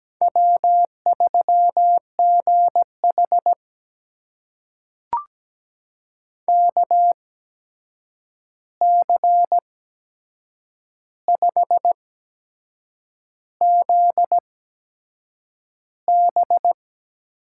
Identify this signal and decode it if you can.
Morse code, heard as a keyed tone, about 17 wpm, spaced as for 7 wpm, 700 Hz. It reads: W3GH KC5ZB
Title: USA Call Signs - 8x Character Spacing - 17wpm